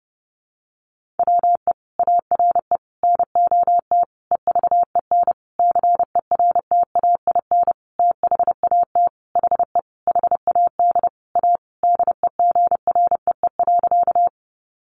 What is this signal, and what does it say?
Morse code, heard as a keyed tone, about 30 wpm, 700 Hz. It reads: WE ARE NOT EVEN CERTAIN THAT HE HAD A DEGREE.
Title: Sentences from Top 1500 Words 30wpm